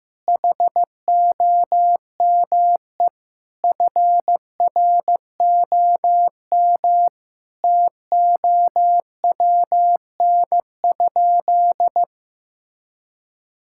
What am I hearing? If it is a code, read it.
Morse code, heard as a keyed tone, about 15 wpm, 700 Hz. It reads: HOME FROM TOWN?
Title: Sentences from Top 1000 Words - Review Part 5 - 15wpm